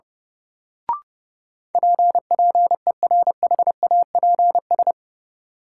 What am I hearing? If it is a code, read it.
Morse code, heard as a keyed tone, about 32 wpm, 700 Hz. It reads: P PERHAPS
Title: Sentences from Top 1100 Words - New Word Drill Alt - 30wpm